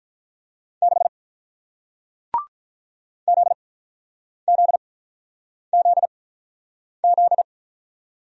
Morse code, heard as a keyed tone, about 48 wpm, spaced as for 9 wpm, 700 Hz. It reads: 6 ZZZZ